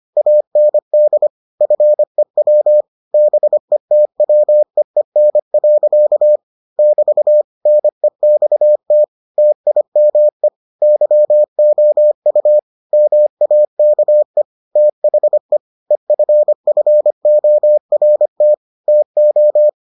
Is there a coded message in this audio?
Yes: Morse code from 0.2 to 19.7 s, heard as a keyed tone, about 25 wpm, 600 Hz. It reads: AND FEW BETWEEN. <BT> NEXT TIME YOU MAKE THE EFFORT TO